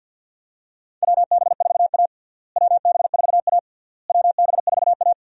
Morse code, heard as a keyed tone, about 46 wpm, 700 Hz. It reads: WB4A WB4A WB4A
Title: USA Call Signs - 4 Characters - Repeated 3x - 50wpm